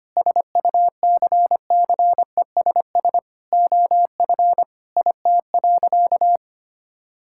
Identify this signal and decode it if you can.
Morse code, heard as a keyed tone, about 25 wpm, 700 Hz. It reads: SUCCESS OF IT.
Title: Sentences from Top 1000 Words - Review Part 6 - 25wpm